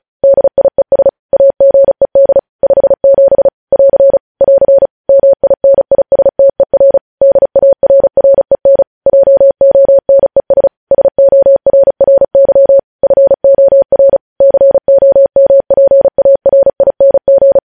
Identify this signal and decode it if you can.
Morse code, heard as a keyed tone, about 36 wpm, 550 Hz. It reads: DIES AGED 57 <AR> <AR> MINISTER DARREN JONES SORRY FOR COMPARING
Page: News headlines in Morse code at 35 wpm